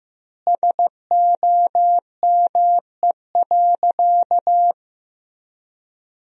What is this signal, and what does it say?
Morse code, heard as a keyed tone, about 15 wpm, 700 Hz. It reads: SOME.
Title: 3x Word-Spacing - Sentences from Top 100 Words - 15wpm